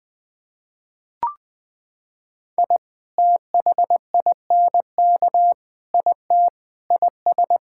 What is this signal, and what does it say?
Morse code, heard as a keyed tone, about 21 wpm, 700 Hz. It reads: I THINK IT IS
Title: Sentences with 8-Word Phrases - 20wpm